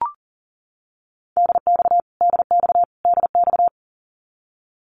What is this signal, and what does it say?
Morse code, heard as a keyed tone, about 40 wpm, 700 Hz. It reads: DX DX DX
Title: Abbreviations Repeated 3x - 40wpm